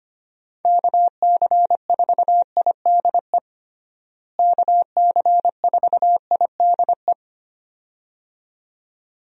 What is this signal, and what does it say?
Morse code, heard as a keyed tone, about 25 wpm, 700 Hz. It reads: KC4IDE KC4IDE
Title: USA Call Signs - 6 Characters - Repeated 2x - 25wpm